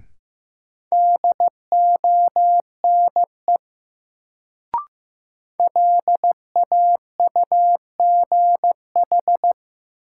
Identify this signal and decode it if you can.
Morse code, heard as a keyed tone, about 15 wpm, 700 Hz. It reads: DONE LAUGH
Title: Sentences from Top 500 Words - New Word Drill - 15wpm